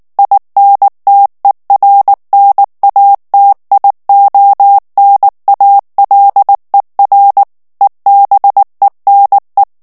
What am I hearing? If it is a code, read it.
Morse code, heard as a keyed tone, about 19 wpm, 800 Hz. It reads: INTERNATIONALER EBENE
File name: morse mp3 / ALLEMAND texte clair 2 vitesse 18 mots minute